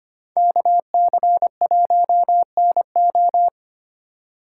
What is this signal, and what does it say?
Morse code, heard as a keyed tone, about 29 wpm, 700 Hz. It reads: KC1NO
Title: Speed Racing - USA 2x2 Call Signs 25wpm